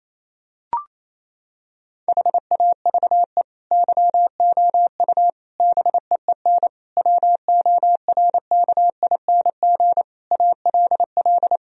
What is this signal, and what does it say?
Morse code, heard as a keyed tone, about 28 wpm, 700 Hz. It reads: HAVE YOU BEEN WORKING ALL
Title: Sentences from Top 1000 Words - Review Part 4 - 28wpm